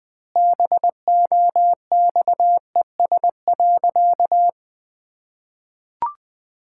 Morse code, heard as a keyed tone, about 20 wpm, 700 Hz. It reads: BOXES.
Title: Pangram Challenge v2 - 20wpm